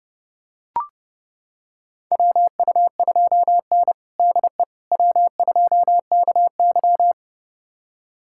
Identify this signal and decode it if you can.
Morse code, heard as a keyed tone, about 30 wpm, 700 Hz. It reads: WU2N DE W2KY
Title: USA Callsign DE Callsign - 4 Characters - 30wpm